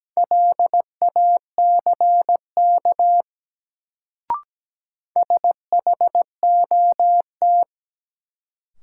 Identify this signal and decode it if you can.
Morse code, heard as a keyed tone, about 17 wpm, 700 Hz. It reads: LACK SHOT
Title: Sentences from Top 1100 Words - New Word Drill - 17wpm